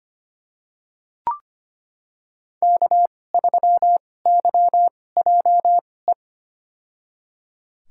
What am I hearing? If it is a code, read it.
Morse code, heard as a keyed tone, about 25 wpm, spaced as for 18 wpm, 700 Hz. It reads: K3YJE
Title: USA Call Signs - 5 Characters - 2x Character Spacing - 25wpm